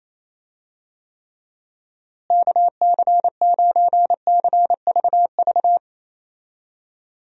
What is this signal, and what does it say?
Morse code, heard as a keyed tone, about 28 wpm, 700 Hz. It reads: KC9CVV
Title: USA Call Signs - 6 Characters - Repeated 3x - 28wpm